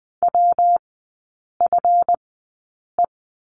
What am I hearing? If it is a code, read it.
Morse code, heard as a keyed tone, about 20 wpm, spaced as for 8 wpm, 700 Hz. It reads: WFE
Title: 178 Lesson 29e - Slash - Three Characters 20wpm